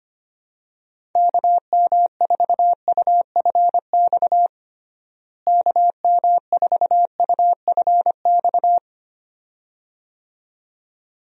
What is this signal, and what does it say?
Morse code, heard as a keyed tone, about 25 wpm, 700 Hz. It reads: KM4UFX KM4UFX
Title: USA Call Signs - 6 Characters - Repeated 2x - 25wpm